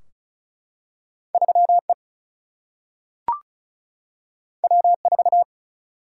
Morse code, heard as a keyed tone, about 36 wpm, 700 Hz. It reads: 3 E W 4
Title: Single-Letter Single-Number 35wpm